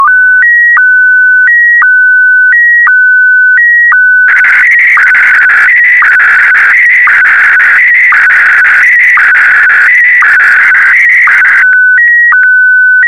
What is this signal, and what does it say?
SSTV (slow-scan television), Scottie DX.
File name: sstv / SDX